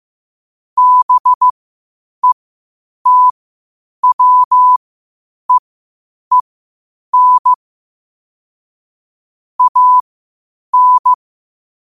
Morse code, heard as a keyed tone, about 15 wpm, spaced as for 8 wpm, 1000 Hz. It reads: BETWEEN AN
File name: qotdfiles / 8wpm